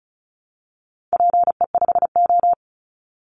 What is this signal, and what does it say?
Morse code, heard as a keyed tone, about 34 wpm, 700 Hz. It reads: PE5O
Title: Call Signs - 35wpm